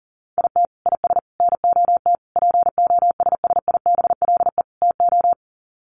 Morse code, heard as a keyed tone, about 40 wpm, 700 Hz. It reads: IT IS NOT POSSIBLE TO
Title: Sets of 5 Words Encore2 40wpm